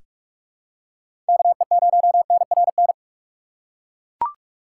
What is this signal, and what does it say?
Morse code, heard as a keyed tone, about 42 wpm, 700 Hz. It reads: KE0NRN
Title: USA Call Signs - 6x Character Spacing - 45wpm